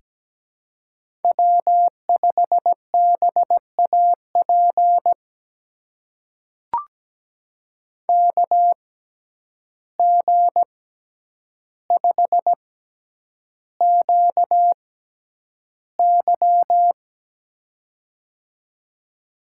Morse code, heard as a keyed tone, about 17 wpm, spaced as for 9 wpm, 700 Hz. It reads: W5BAP KG5QY
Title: USA Call Signs - 6x Character Spacing - 17wpm